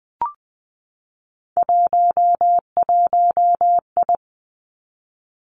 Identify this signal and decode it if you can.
Morse code, heard as a keyed tone, about 19 wpm, 700 Hz. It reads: 11I
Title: Three Challenging Letter-Number Combinations 20wpm